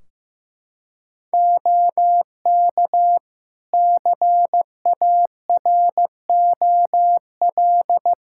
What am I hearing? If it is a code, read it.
Morse code, heard as a keyed tone, about 15 wpm, 700 Hz. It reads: OK CAROL
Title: Practice QSO - Exchange 4 - Ending - 15wpm